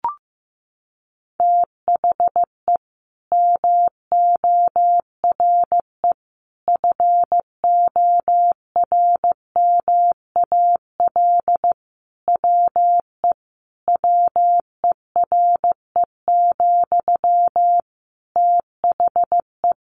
Morse code, heard as a keyed tone, about 15 wpm, 700 Hz. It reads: THE MORE FORMAL WE WERE, THE
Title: Sentences from Top 1300 Words 15wpm